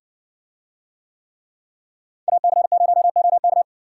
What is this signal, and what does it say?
Morse code, heard as a keyed tone, about 60 wpm, 700 Hz. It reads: AC0OK